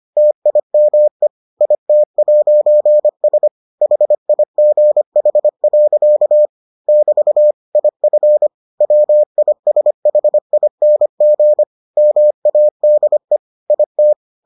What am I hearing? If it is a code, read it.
Morse code, heard as a keyed tone, about 25 wpm, 600 Hz. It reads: TIME IT'S HIGH. <BT> IF WISHING MADE IT